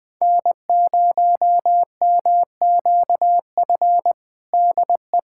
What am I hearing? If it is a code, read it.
Morse code, heard as a keyed tone, about 20 wpm, 700 Hz. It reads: N0MQF DE
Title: USA Callsign DE Callsign 20wpm